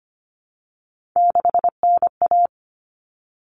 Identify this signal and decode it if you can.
Morse code, heard as a keyed tone, about 25 wpm, 700 Hz. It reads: 6NA
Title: Common Cut Numbers - 3-Digit 25wpm